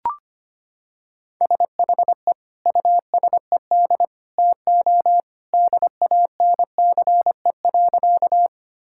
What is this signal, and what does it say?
Morse code, heard as a keyed tone, about 25 wpm, 700 Hz. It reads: SHE USED TO DANCE.
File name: practice / 25wpm / Sentences with 2-Word Phrases - 25wpm